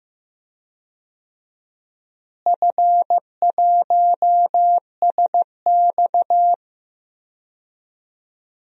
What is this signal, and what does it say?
Morse code, heard as a keyed tone, about 15 wpm, 700 Hz. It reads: F1SX
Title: DXCC-100 Call Signs - 4 Characters - Repeated 3x - 15wpm